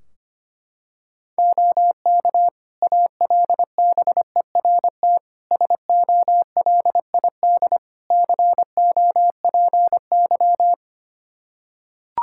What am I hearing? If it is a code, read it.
Morse code, heard as a keyed tone, about 25 wpm, 700 Hz. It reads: OK ALBERT SOLID COPY
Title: Practice QSO - Exchange 3 - More Info - 25wpm